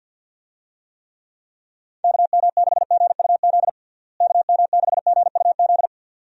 Morse code, heard as a keyed tone, about 50 wpm, 700 Hz. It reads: KM6GUZ KM6GUZ
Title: USA Call Signs - 6 Characters - Repeated 2x - 50wpm